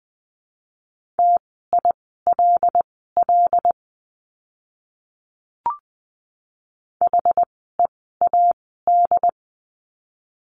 Morse code, heard as a keyed tone, about 20 wpm, spaced as for 15 wpm, 700 Hz. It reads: TILL HEAD
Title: 075 Lesson 13f - Letter L - Words 20wpm